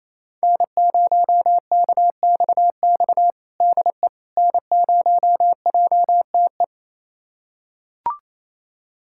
Morse code, heard as a keyed tone, about 28 wpm, 700 Hz. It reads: N0KXX DE N0JTE
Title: USA Callsign DE Callsign - 5 Characters - 28wpm